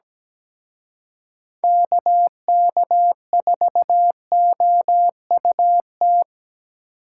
Morse code, heard as a keyed tone, about 20 wpm, 700 Hz. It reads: KK4OUT
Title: Speed Racing - USA 2x3 Call Signs 17wpm